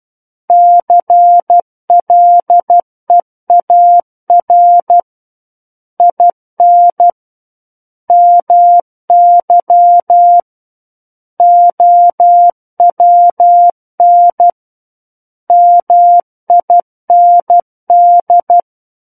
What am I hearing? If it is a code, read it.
Morse code, heard as a keyed tone, about 12 wpm, 700 Hz. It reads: CLEAR IN MY OWN MIND
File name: Book One / 03-Chapter 3 - 12 WPM